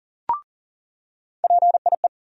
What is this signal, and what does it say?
Morse code, heard as a keyed tone, about 40 wpm, 700 Hz. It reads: PIE
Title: USA Call Sign Postfixes - 3 Letters 40wpm